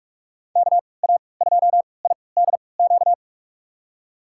Morse code, heard as a keyed tone, about 46 wpm, spaced as for 26 wpm, 700 Hz. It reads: KA2IDQ
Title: USA Call Signs - 6 Characters - 3x Character Spacing - 45wpm